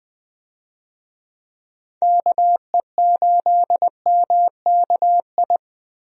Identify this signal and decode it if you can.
Morse code, heard as a keyed tone, about 20 wpm, 700 Hz. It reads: KE8MKI
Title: USA Call Signs - 6 Characters - Repeated 3x - 20wpm